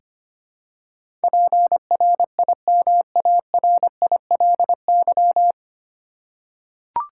Morse code, heard as a keyed tone, about 25 wpm, 700 Hz. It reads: PRIMARILY
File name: practice / 25wpm / 9 Letter Words - First 3 Letters Given - 25wpm